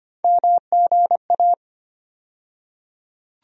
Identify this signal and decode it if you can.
Morse code, heard as a keyed tone, about 25 wpm, 700 Hz. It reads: MGA
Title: USA Call Sign Postfixes - 3 Letters 25wpm